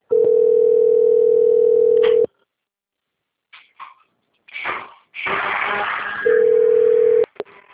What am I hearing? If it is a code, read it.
Ringback tone (North America), 440 + 480 Hz.